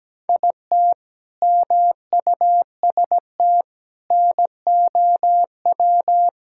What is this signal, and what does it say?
Morse code, heard as a keyed tone, about 17 wpm, 700 Hz. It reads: IT MUST NOW
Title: Sentences from Top 1000 Words - Review Part 5 - 17wpm